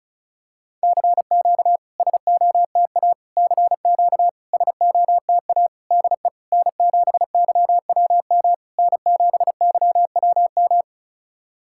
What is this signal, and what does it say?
Morse code, heard as a keyed tone, about 36 wpm, 700 Hz. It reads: CQ SOTA CQ SOTA DE N7YWM N7YWM
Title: SOTA QSO Practice 35wpm